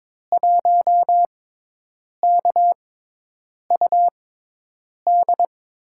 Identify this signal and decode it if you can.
Morse code, heard as a keyed tone, about 22 wpm, spaced as for 8 wpm, 700 Hz. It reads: 1KUD
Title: USA Call Signs - 5 Characters - 6x Character Spacing - 22wpm